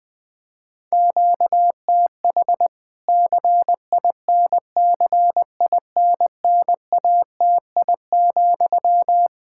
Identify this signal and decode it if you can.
Morse code, heard as a keyed tone, about 21 wpm, 700 Hz. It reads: QTH CINCINNATI,
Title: Practice QSO - Exchange 1 - Beginning - 20wpm